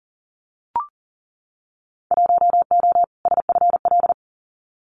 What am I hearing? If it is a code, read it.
Morse code, heard as a keyed tone, about 40 wpm, 700 Hz. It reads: 1O SFL
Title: Winter Field Day Exchanges Encore 40wpm